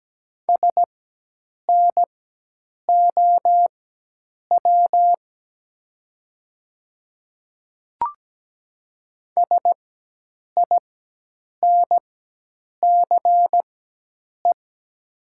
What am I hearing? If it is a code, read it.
Morse code, heard as a keyed tone, about 17 wpm, spaced as for 8 wpm, 700 Hz. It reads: SNOW SINCE
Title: 4x Character Spacing - Top 500 Words - 17wpm